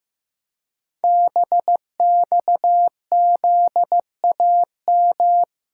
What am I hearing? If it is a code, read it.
Morse code, heard as a keyed tone, about 15 wpm, 700 Hz. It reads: BXZAM
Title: Code Groups - 5 Letters - 15wpm